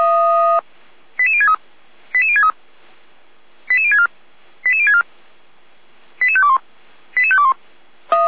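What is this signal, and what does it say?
Five-tone selective call (ZVEI), heard as quick runs of five tones, tones of about 70 ms. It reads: 88063 88063 88064 88064 88521 88521
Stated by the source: ZVEI five-tone call received by SDR, from the Feuerwehr Orlamünde fire brigade server.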